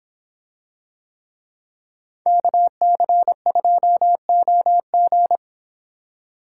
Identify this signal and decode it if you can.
Morse code, heard as a keyed tone, about 26 wpm, 700 Hz. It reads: KC2OG